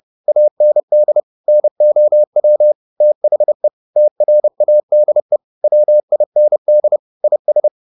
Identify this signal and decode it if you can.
Morse code, heard as a keyed tone, about 30 wpm, 600 Hz. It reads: AND NOW THE TRADE WIND IS